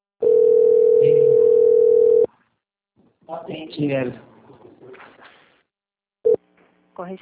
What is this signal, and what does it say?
Ringback tone (North America), 440 + 480 Hz, with speech.
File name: rg-1600-unknown- - - .38737